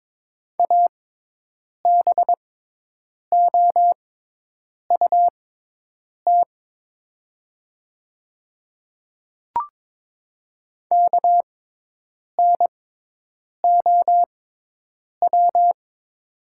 Morse code, heard as a keyed tone, about 23 wpm, spaced as for 8 wpm, 700 Hz. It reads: ABOUT KNOW
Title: 6x Character Spacing - Top 100 Words - 22wpm